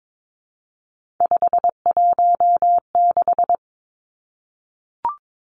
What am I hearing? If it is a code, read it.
Morse code, heard as a keyed tone, about 22 wpm, 700 Hz. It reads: 516
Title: Numbers - 3-Digit 22wpm